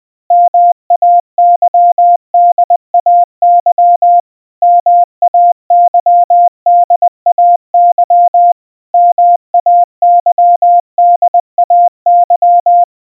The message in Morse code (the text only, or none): MAYDAY MAYDAY MAYDAY